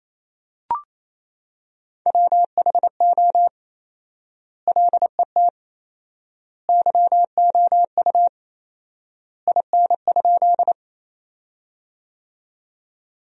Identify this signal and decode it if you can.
Morse code, heard as a keyed tone, about 28 wpm, 700 Hz. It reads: WHO LET YOU IN?
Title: 4x Word-Spacing - Sentences from Top 300 Words - 28wpm